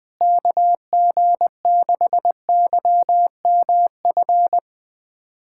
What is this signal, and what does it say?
Morse code, heard as a keyed tone, about 21 wpm, 700 Hz. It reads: KG6YMF
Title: USA Callsign DE Callsign 20wpm